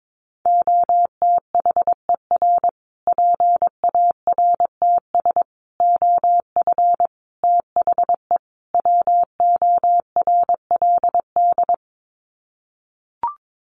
Morse code, heard as a keyed tone, about 22 wpm, 700 Hz. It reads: OTHER PARTS OF THE WORLD
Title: Sets of 5 Words Encore 22wpm